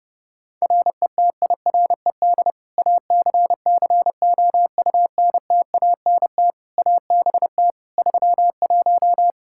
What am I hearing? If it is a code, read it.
Morse code, heard as a keyed tone, about 30 wpm, 700 Hz. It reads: RETIRED ACCOUNTANT ABT 31
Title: K1USN SST QSO Practice 30wpm